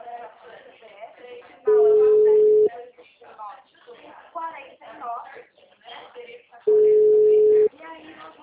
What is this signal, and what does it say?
Ringback tone (Europe, 425 Hz), with speech.